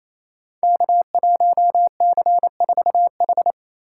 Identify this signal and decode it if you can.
Morse code, heard as a keyed tone, about 28 wpm, 700 Hz. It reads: K1C4H
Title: Code Groups - 5 Letter:Numbers - 28wpm